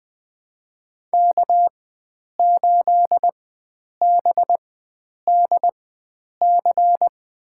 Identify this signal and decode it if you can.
Morse code, heard as a keyed tone, about 20 wpm, spaced as for 9 wpm, 700 Hz. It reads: K8BDC
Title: USA Call Signs - 5 Characters - 4x Character Spacing - 20wpm